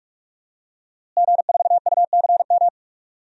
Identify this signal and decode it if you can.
Morse code, heard as a keyed tone, about 46 wpm, 700 Hz. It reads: G4UCM
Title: DXCC-100 Call Signs - 5 Characters - Repeated 2x - 45wpm